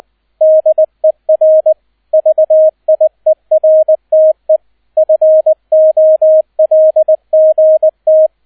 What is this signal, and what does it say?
Morse code, heard as a keyed tone, about 20 wpm, 630 Hz, with radio noise. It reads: DER VIERTE FOLGT